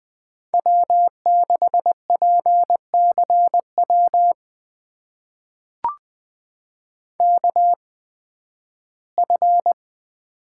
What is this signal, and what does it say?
Morse code, heard as a keyed tone, about 20 wpm, spaced as for 13 wpm, 700 Hz. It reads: W6PCW KF
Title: USA Call Signs - 5 Characters - 8x Character Spacing - 20wpm